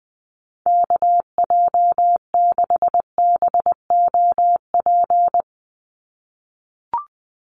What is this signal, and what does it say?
Morse code, heard as a keyed tone, about 20 wpm, 700 Hz. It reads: KJ6BOP
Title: USA Call Signs 20wpm